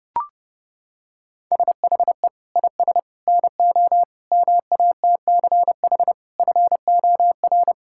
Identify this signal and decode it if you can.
Morse code, heard as a keyed tone, about 32 wpm, 700 Hz. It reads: SHE IS NO MATCH FOR
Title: Sentences from Top 1600 Words 30wpm